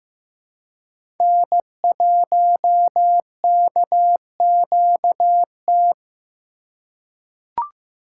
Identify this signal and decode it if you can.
Morse code, heard as a keyed tone, about 15 wpm, 700 Hz. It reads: N1KQT
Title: USA Call Signs - 5 Characters - 4x Character Spacing - 15wpm